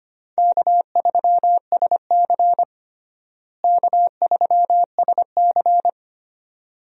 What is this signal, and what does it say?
Morse code, heard as a keyed tone, about 25 wpm, 700 Hz. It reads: K3SC K3SC
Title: USA Call Signs - 4 Characters - Repeated 2x - 25wpm